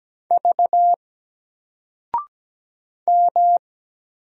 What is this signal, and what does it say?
Morse code, heard as a keyed tone, about 17 wpm, 700 Hz. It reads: V M
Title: Single Letter-Number 17wpm